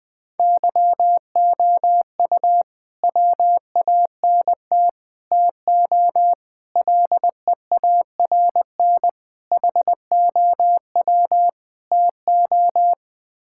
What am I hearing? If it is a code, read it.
Morse code, heard as a keyed tone, about 20 wpm, 700 Hz. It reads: YOU WANT TO LEARN HOW TO
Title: Sets of 6 Words - 20wpm